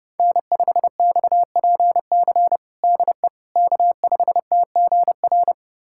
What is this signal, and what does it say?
Morse code, heard as a keyed tone, about 30 wpm, 700 Hz. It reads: N5XPC DE K5TGR
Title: USA Callsign DE Callsign - 5 Characters - 30wpm